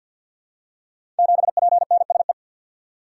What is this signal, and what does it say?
Morse code, heard as a keyed tone, about 52 wpm, 700 Hz. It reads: 7PNSE
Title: Code Groups - 5 Letter:Numbers - 50wpm